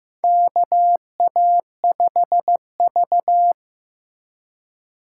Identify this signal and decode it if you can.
Morse code, heard as a keyed tone, about 15 wpm, 700 Hz. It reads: KA5V
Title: USA Call Signs - 4 Characters - 2x Character Spacing - 15wpm